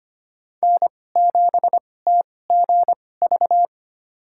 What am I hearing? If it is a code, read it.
Morse code, heard as a keyed tone, about 25 wpm, spaced as for 18 wpm, 700 Hz. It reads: N7TGV
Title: USA Call Signs - 5 Characters - 2x Character Spacing - 25wpm